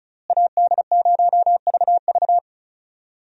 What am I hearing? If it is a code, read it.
Morse code, heard as a keyed tone, about 34 wpm, 700 Hz. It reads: AD0VV
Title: USA Call Signs - 5 Characters - Repeated 2x - 35wpm